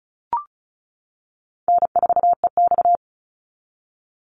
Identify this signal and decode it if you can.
Morse code, heard as a keyed tone, about 34 wpm, 700 Hz. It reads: N4EX
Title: NA SOTA Chaser Callsigns 35wpm